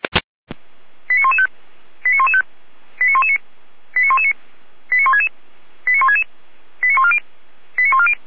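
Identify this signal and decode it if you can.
Five-tone selective call (ZVEI), heard as quick runs of five tones, tones of about 80 ms. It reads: 89115 89115 89119 89119 89166 89166 89140 89140
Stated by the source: ZVEI five-tone call received by SDR.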